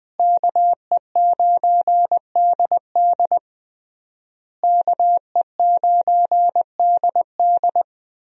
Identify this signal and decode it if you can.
Morse code, heard as a keyed tone, about 20 wpm, 700 Hz. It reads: KE9DD KE9DD
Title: USA Call Signs - 5 Characters - Repeated 3x - 20wpm